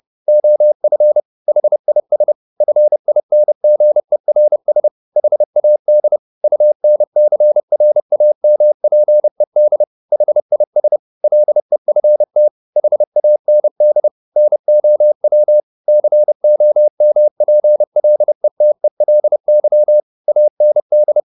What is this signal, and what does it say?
Morse code, heard as a keyed tone, about 30 wpm, 600 Hz. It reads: OF HIS FINGERS HAD UNCRAMPED HIS LEFT HAND NOW COMPLETELY AND
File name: the old man and the sea / the old man and the sea 1 30wpm